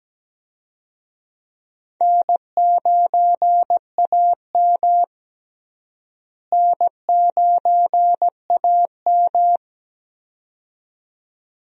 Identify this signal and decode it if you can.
Morse code, heard as a keyed tone, about 17 wpm, 700 Hz. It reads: N9AM N9AM
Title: USA Call Signs - 4 Characters - Repeated 2x - 17wpm